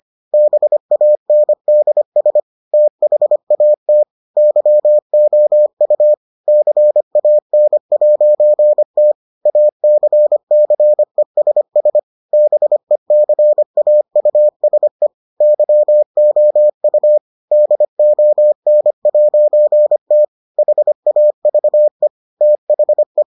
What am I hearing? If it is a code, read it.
Morse code, heard as a keyed tone, about 25 wpm, 600 Hz. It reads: BANDS THAT YOU CAN'T ACCESS BECAUSE YOU DON'T HAVE THE